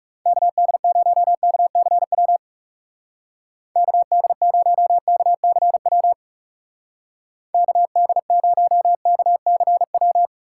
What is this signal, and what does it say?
Morse code, heard as a keyed tone, about 35 wpm, 700 Hz. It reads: KD0KCW KD0KCW KD0KCW
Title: Speed Racing - USA 2x3 Call Signs 30wpm